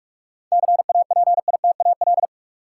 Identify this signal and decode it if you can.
Morse code, heard as a keyed tone, about 46 wpm, 700 Hz. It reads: CAPITAL
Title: Sentences from Top 1000 Words - New Word Drill - 45wpm